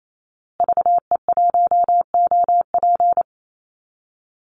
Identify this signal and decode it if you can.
Morse code, heard as a keyed tone, about 28 wpm, 690 Hz. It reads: VE1OP
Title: Call Signs Repeated 3x - 28wpm